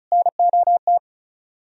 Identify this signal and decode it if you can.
Morse code, heard as a keyed tone, about 43 wpm, 700 Hz. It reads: NOT
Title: Speed-Racing - Sets of 4 Words 35wpm